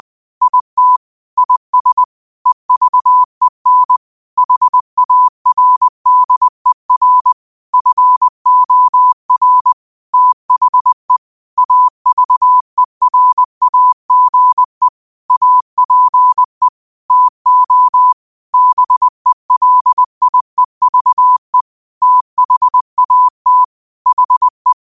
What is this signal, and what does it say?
Morse code, heard as a keyed tone, about 20 wpm, 1000 Hz. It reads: IT IS EVEN HARDER FOR THE AVERAGE APE TO BELIEVE THAT HE
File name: qotdfiles / 20wpm